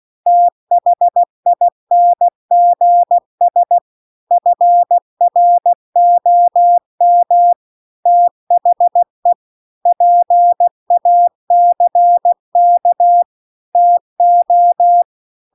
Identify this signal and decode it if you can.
Morse code, heard as a keyed tone, about 16 wpm, 700 Hz. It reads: THINGS FROM THE PACK TO